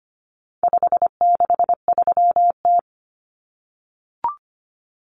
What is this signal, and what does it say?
Morse code, heard as a keyed tone, about 25 wpm, 700 Hz. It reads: 563T
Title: Common Cut Numbers - 4-Digit 25wpm